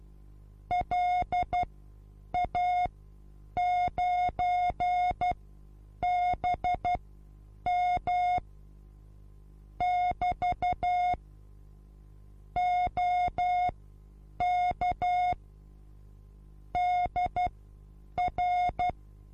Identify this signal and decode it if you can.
Morse code, heard as a keyed tone, about 12 wpm, spaced as for 8 wpm, 720 Hz. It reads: LA9BM <BT> OK DR
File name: Onlinelehrgang / morsen / DJ4UF Morsen23